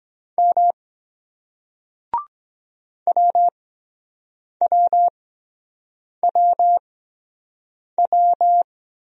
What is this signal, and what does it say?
Morse code, heard as a keyed tone, about 23 wpm, spaced as for 7 wpm, 700 Hz. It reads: M WWWW